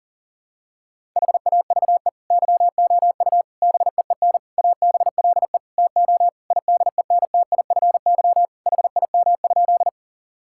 Morse code, heard as a keyed tone, about 37 wpm, 700 Hz. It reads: HAVE YOU BEEN ABLE TO IDENTIFY HIM?
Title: Sentences from Top 1000 Words - Review Part 6 - 40wpm